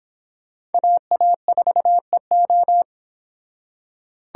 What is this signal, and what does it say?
Morse code, heard as a keyed tone, about 28 wpm, 700 Hz. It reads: AA4EO